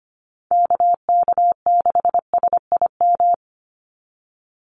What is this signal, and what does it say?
Morse code, heard as a keyed tone, about 25 wpm, 700 Hz. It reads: KK6SIM
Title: USA Call Signs 25wpm